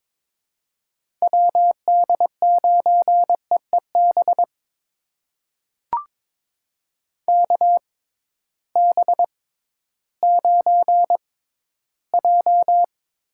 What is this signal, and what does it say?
Morse code, heard as a keyed tone, about 22 wpm, spaced as for 13 wpm, 700 Hz. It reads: WD9EEB KB9J
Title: USA Call Signs - 6 Characters - 6x Character Spacing - 22wpm